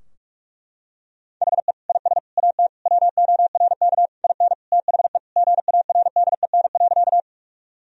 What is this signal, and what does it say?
Morse code, heard as a keyed tone, about 42 wpm, 700 Hz. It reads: HE IS AT WORK IN THE GARDEN.
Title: Sentences from Top 1000 Words - Review Part 4 - 45wpm